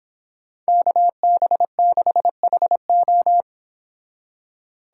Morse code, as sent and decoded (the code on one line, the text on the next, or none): -.- -... -.... .... ---
KB6HO